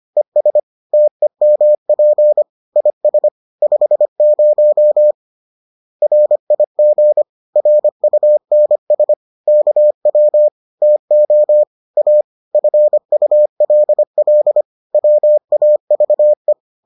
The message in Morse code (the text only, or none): ES TEMP IS 50 RIG RUNS KW TO A FULL WAVE